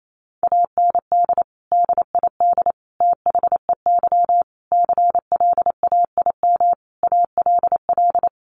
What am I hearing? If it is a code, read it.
Morse code, heard as a keyed tone, about 28 wpm, 700 Hz. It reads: AND DID THEY CLAIM ALL
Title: Sentences from Top 1500 Words 28wpm